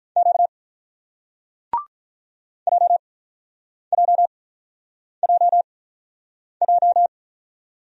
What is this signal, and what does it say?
Morse code, heard as a keyed tone, about 47 wpm, spaced as for 9 wpm, 700 Hz. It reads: Q JJJJ